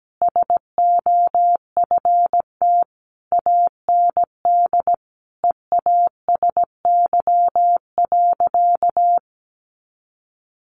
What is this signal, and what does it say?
Morse code, heard as a keyed tone, about 17 wpm, 700 Hz. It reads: SOFT AND EASY.
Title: Sentences from Top 1500 Words 17wpm